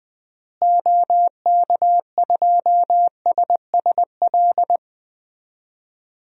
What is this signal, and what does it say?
Morse code, heard as a keyed tone, about 20 wpm, 700 Hz. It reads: OK2SSL
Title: DXCC-100 Call Signs - 6 Characters - Repeated 2x - 20wpm